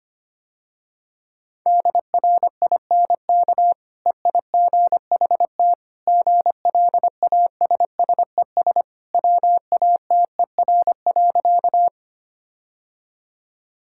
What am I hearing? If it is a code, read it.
Morse code, heard as a keyed tone, about 25 wpm, 700 Hz. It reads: DRINK EIGHT GLASSES WATER.
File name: Easy Sentences with Hints - 25wpm